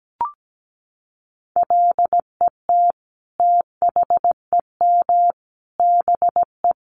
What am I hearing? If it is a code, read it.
Morse code, heard as a keyed tone, about 17 wpm, 700 Hz. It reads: LET THEM BE
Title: Sentences from Top 1300 Words 17wpm